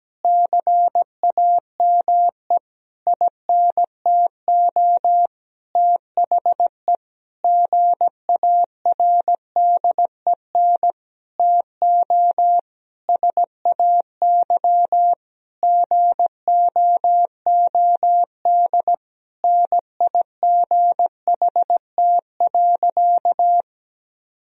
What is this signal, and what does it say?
Morse code, heard as a keyed tone, about 17 wpm, 700 Hz. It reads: CAME INTO THE GARDEN TO SAY GOOD NIGHT.
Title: Sentences from Top 1000 Words - Review Part 4 - 17wpm